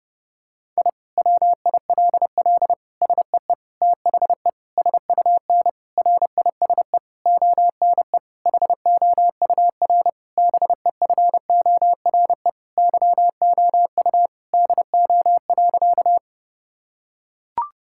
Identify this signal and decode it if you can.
Morse code, heard as a keyed tone, about 32 wpm, 700 Hz. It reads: I WILL SEE THE SUN RISE ONE HOUR BEFORE YOU DO.
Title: Sentences from Top 1000 Words - Review Part 6 - 30wpm